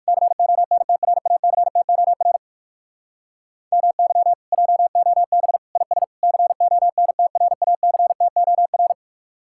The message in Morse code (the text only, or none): CONTRACTOR MY JOB IS CONTRACTOR